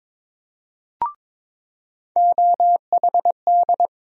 Morse code, heard as a keyed tone, about 22 wpm, 700 Hz. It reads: OHD
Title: USA Call Sign Postfixes - 3 Letters 22wpm